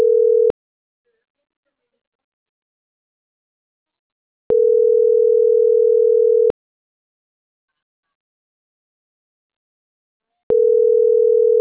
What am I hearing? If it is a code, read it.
Ringback tone (North America), 440 + 480 Hz.